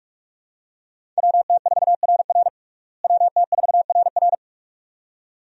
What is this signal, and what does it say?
Morse code, heard as a keyed tone, about 45 wpm, 700 Hz. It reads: WT4RR WT4RR
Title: USA Call Signs - 5 Characters - Repeated 2x - 45wpm